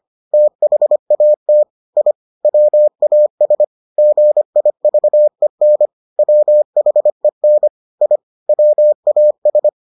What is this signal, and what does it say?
Morse code, heard as a keyed tone, about 25 wpm, 600 Hz. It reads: THAT I WAS GIVEN WHEN I WAS